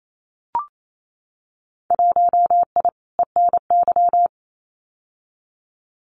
Morse code, heard as a keyed tone, about 28 wpm, 700 Hz. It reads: 1I ENY
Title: Winter Field Day Exchanges Encore 28wpm